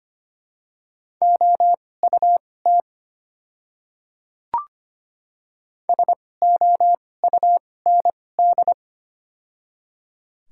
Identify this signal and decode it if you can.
Morse code, heard as a keyed tone, about 25 wpm, spaced as for 18 wpm, 700 Hz. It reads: OUT SOUND
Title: 2x Character Spacing - Top 100 Words - 25wpm